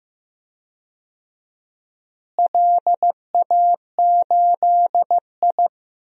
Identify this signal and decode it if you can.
Morse code, heard as a keyed tone, about 15 wpm, 700 Hz. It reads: LA8I
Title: DXCC-100 Call Signs - 4 Characters - Repeated 2x - 15wpm